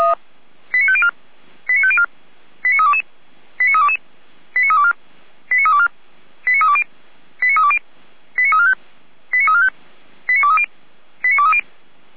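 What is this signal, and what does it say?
Five-tone selective call (ZVEI), heard as quick runs of five tones, tones of about 70 ms. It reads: 89504 89504 89322 89322 89324 89324 89320 89320 89345 89345 89230 89230
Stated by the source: ZVEI five-tone selective call 89280, received by SDR.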